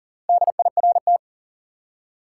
Morse code, heard as a keyed tone, about 37 wpm, 700 Hz. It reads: DIRT
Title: Four Letter Words - 40wpm